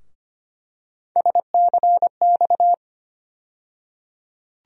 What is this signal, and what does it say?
Morse code, heard as a keyed tone, about 25 wpm, 700 Hz. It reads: SCX
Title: USA Call Sign Postfixes - 3 Letters 25wpm